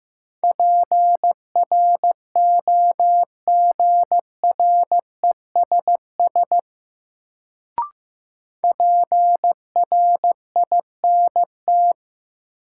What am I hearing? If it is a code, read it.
Morse code, heard as a keyed tone, about 15 wpm, 700 Hz. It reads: PROGRESS PRINT
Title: Sentences from Top 1400 Words - New Word Drill Alt - 15wpm